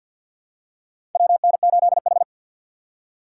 Morse code, heard as a keyed tone, about 46 wpm, 700 Hz. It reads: WN8H
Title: USA Call Signs - 4 Characters - 3x Character Spacing - 50wpm